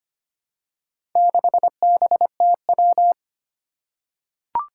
Morse code, heard as a keyed tone, about 25 wpm, 700 Hz. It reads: 6BTW
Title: Four Challenging Letter-Number Combinations 25wpm